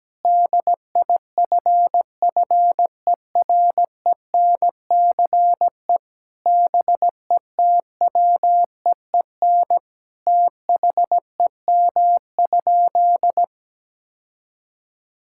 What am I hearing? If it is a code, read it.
Morse code, heard as a keyed tone, about 17 wpm, 700 Hz. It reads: DIFFERENCE BETWEEN THEM?
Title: Sentences from Top 1000 Words - Review Part 5 - 17wpm